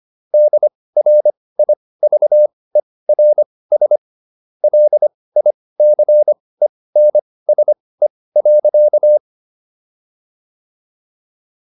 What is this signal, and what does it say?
Morse code, heard as a keyed tone, about 25 wpm, spaced as for 18 wpm, 600 Hz. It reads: DRIVERS LICENSE.